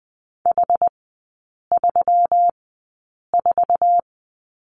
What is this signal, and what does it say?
Morse code, heard as a keyed tone, about 20 wpm, spaced as for 8 wpm, 700 Hz. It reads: H34
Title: 121 Lesson 20e - Number 3 - Three Characters 20wpm